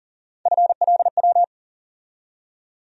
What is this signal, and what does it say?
Morse code, heard as a keyed tone, about 40 wpm, 700 Hz. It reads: FLW
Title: USA Call Sign Postfixes - 3 Letters 40wpm